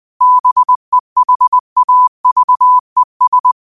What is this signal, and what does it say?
Morse code, heard as a keyed tone, about 20 wpm, 1000 Hz. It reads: BEHAVES
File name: qotdfiles / 20wpm